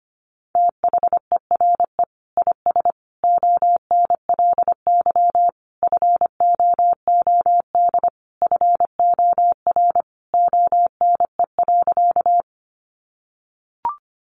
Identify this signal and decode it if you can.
Morse code, heard as a keyed tone, about 25 wpm, 700 Hz. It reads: THERE IS ONLY FOOD FOR ONE.
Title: Sentences from Top 300 Words 25wpm